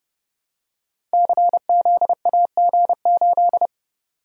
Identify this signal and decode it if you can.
Morse code, heard as a keyed tone, about 30 wpm, 700 Hz. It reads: CZAG8
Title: Code Groups - 5 Letter:Numbers - 30wpm